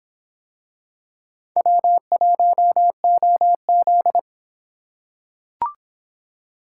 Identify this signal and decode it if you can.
Morse code, heard as a keyed tone, about 26 wpm, 700 Hz. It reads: W1OZ